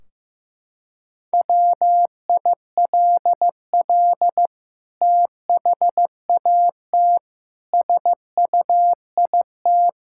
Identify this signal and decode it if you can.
Morse code, heard as a keyed tone, about 15 wpm, 700 Hz. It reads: WILL THAT SUIT
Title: Sentences from Top 1600 Words 15wpm